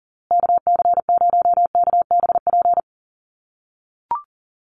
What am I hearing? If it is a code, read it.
Morse code, heard as a keyed tone, about 40 wpm, 700 Hz. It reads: KC0KBP
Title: USA Call Signs Repeated 3x 40wpm